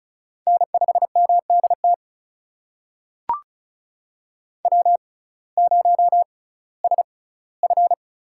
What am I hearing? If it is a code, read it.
Morse code, heard as a keyed tone, about 36 wpm, spaced as for 19 wpm, 700 Hz. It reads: N5MDT W0SF
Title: USA Call Signs - 6x Character Spacing - 35wpm